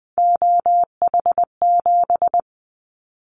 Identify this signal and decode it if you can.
Morse code, heard as a keyed tone, about 20 wpm, 700 Hz. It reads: OH7
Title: Callsign Prefixes for Top 50 DXCC - 20wpm